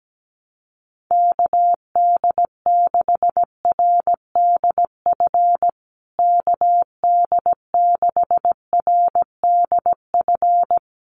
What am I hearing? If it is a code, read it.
Morse code, heard as a keyed tone, about 17 wpm, 700 Hz. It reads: KD6RDF KD6RDF
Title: USA Call Signs Repeated 3x 17wpm